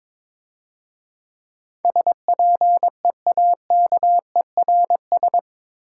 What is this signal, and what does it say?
Morse code, heard as a keyed tone, about 25 wpm, 700 Hz. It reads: SPEAKERS
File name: Easy Words - Speed-Racing with Hints - 22wpm